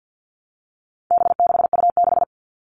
Morse code, heard as a keyed tone, about 50 wpm, 700 Hz. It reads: B6U6
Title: All Cut Numbers - 4-Digit 50wpm